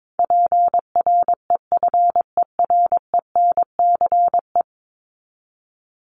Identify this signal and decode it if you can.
Morse code, heard as a keyed tone, about 22 wpm, 700 Hz. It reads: PREFERENCE
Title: Top 1251-1500 Words 22wpm